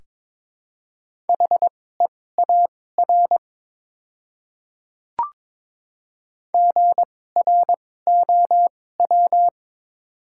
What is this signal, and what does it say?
Morse code, heard as a keyed tone, about 22 wpm, spaced as for 16 wpm, 700 Hz. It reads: HEAR GROW
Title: 2x Character Spacing - Top 300 Words - 22wpm